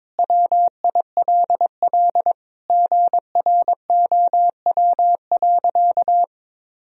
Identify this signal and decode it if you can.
Morse code, heard as a keyed tone, about 21 wpm, 700 Hz. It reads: WILL GROW.
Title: Sentences from Top 1000 Words - Review Part 2 - 22wpm